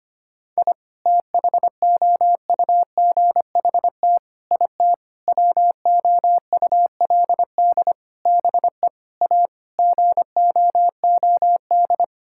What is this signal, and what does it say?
Morse code, heard as a keyed tone, about 25 wpm, 700 Hz. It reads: I THOUGHT IT WOULD BE A GOOD
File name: practice / 25wpm / Sentences with 9-Word Phrases - 25wpm